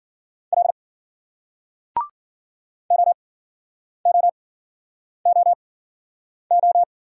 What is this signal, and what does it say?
Morse code, heard as a keyed tone, about 51 wpm, spaced as for 9 wpm, 700 Hz. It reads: L OOOO